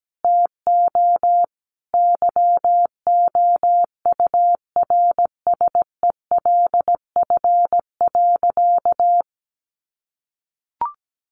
Morse code, heard as a keyed tone, about 17 wpm, 700 Hz. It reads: TO YOURSELF.
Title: Sentences from Top 800 Words 17wpm